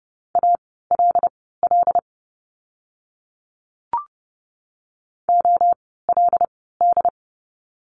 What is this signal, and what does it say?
Morse code, heard as a keyed tone, about 30 wpm, spaced as for 17 wpm, 700 Hz. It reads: ALL OLD
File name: course / 30:10wpm / 230 Lesson 37f - Letter x - Words 30wpm